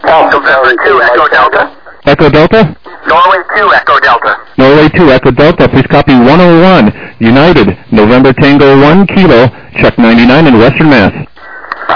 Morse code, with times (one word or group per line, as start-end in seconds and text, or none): none